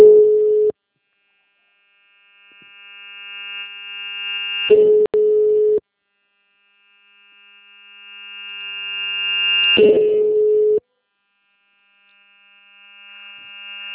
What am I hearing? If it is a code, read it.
Ringback tone (Europe, 425 Hz).